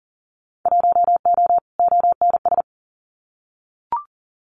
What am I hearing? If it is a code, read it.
Morse code, heard as a keyed tone, about 40 wpm, 700 Hz. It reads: 1O ONS
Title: Winter Field Day Exchanges 40wpm